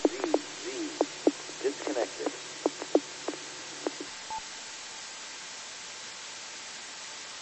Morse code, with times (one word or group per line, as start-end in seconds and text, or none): none